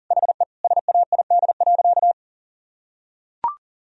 Morse code, heard as a keyed tone, about 40 wpm, 700 Hz. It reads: HE SAID.
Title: Sentences from Top 1000 Words - Review Part 1 - 40wpm